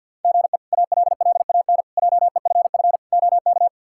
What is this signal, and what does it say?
Morse code, heard as a keyed tone, about 47 wpm, 700 Hz. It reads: GE ALLAN JEFF OK
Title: K1USN SST QSO Practice - 50wpm